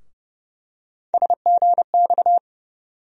Morse code, heard as a keyed tone, about 30 wpm, 700 Hz. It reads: SGX
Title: USA Call Sign Postfixes - 3 Letters 30wpm